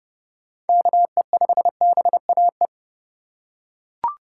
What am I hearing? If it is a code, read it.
Morse code, heard as a keyed tone, about 30 wpm, 700 Hz. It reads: KE5BAE
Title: USA Call Signs - 6 Characters - 4x Character Spacing - 30wpm